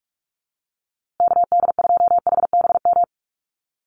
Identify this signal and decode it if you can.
Morse code, heard as a keyed tone, about 44 wpm, 700 Hz. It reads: KD2HBM
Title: USA Call Signs Repeated 3x 45wpm